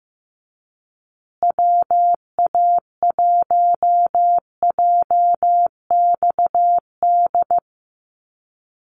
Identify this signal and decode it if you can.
Morse code, heard as a keyed tone, about 15 wpm, 700 Hz. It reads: WA1JXD
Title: USA Call Signs Repeated 2x 15wpm